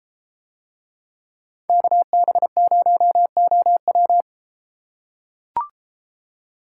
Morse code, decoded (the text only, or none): KB0OW